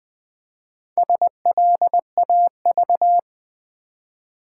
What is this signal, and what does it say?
Morse code, heard as a keyed tone, about 20 wpm, 700 Hz. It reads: SLAV
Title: Build A Word - 1 Letter At A Time - 5-Letter Words - 20wpm